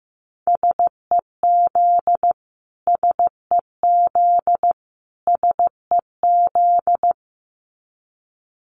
Morse code, heard as a keyed tone, about 15 wpm, 700 Hz. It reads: SEZ SEZ SEZ
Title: Abbreviations Repeated 3x - 15wpm